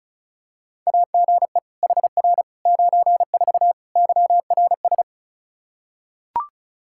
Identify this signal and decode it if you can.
Morse code, heard as a keyed tone, about 34 wpm, 700 Hz. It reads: AGE HR 94 YRS
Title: K1USN SST QSO Practice 35wpm